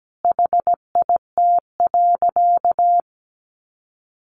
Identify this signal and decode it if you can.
Morse code, heard as a keyed tone, about 17 wpm, 700 Hz. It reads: HIT.
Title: Sentences from Top 1200 Words 17wpm